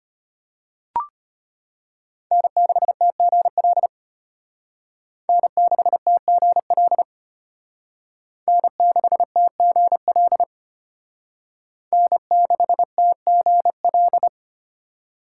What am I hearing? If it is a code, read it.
Morse code, heard as a keyed tone, about 32 wpm, 700 Hz. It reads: N6TGL N6TGL N6TGL N6TGL